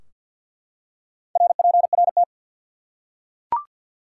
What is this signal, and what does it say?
Morse code, heard as a keyed tone, about 50 wpm, 700 Hz. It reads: RPRT
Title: Top 90 QSO Elements 50wpm